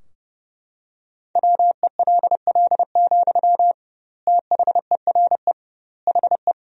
Morse code, heard as a keyed tone, about 30 wpm, 700 Hz. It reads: WELL, THERE HE
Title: 2x Word-Spacing - Sentences from Top 500 Words - 30wpm